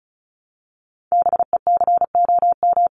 Morse code, heard as a keyed tone, about 34 wpm, 700 Hz. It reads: BECOM
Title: ING Suffix - Top 100 Words with Base Words 35wpm